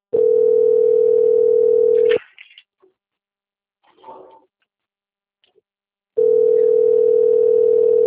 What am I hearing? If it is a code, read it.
Ringback tone (North America), 440 + 480 Hz.
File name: out- - - - - .315